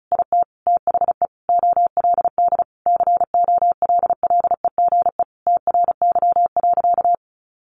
Morse code, heard as a keyed tone, about 34 wpm, 700 Hz. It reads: IT THE OLD COLLEGE TRY.
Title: Sentences from Top 1300 Words 35wpm